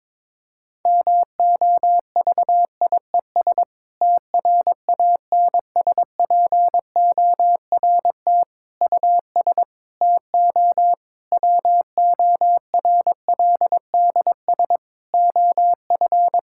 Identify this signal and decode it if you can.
Morse code, heard as a keyed tone, about 22 wpm, 700 Hz. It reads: MOVIES TRANSPORT US TO WORLDS OF
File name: Difficult Sentences with Hints - 22wpm